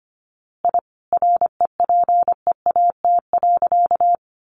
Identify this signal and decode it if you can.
Morse code, heard as a keyed tone, about 25 wpm, 700 Hz. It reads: I REPEAT.
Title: A Sound Of Thunder 25wpm